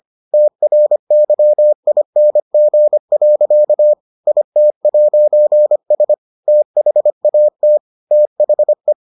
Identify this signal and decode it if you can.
Morse code, heard as a keyed tone, about 25 wpm, 600 Hz. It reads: TRYING. IT'S THAT THE